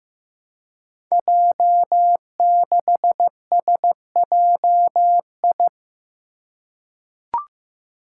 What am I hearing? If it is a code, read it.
Morse code, heard as a keyed tone, about 15 wpm, 700 Hz. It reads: J6SJI
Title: Five Challenging Letter-Number Combinations 15wpm